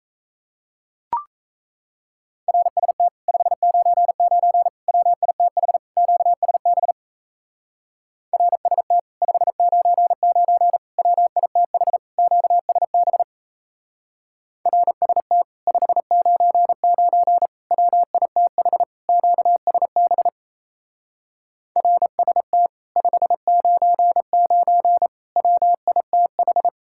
Morse code, heard as a keyed tone, about 36 wpm, 700 Hz. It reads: RST 599 WITH QSB RST 599 WITH QSB RST 599 WITH QSB RST 599 WITH